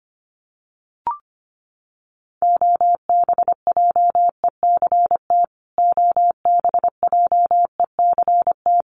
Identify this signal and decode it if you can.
Morse code, heard as a keyed tone, about 25 wpm, 700 Hz. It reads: OBJECT OBJECT
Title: Top 500 Words Repeated 25wpm